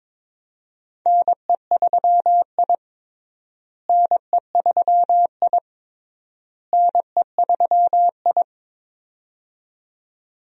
Morse code, heard as a keyed tone, about 23 wpm, 700 Hz. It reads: NE3I NE3I NE3I
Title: USA Call Signs - 4 Characters - Repeated 3x - 22wpm